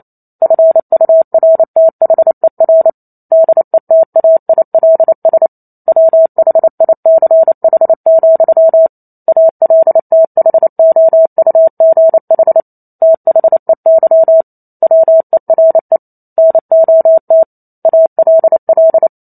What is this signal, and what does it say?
Morse code, heard as a keyed tone, about 28 wpm, 650 Hz. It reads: FURTHER DETAILS WHICH, ALTHOUGH THEY WERE NOT ALL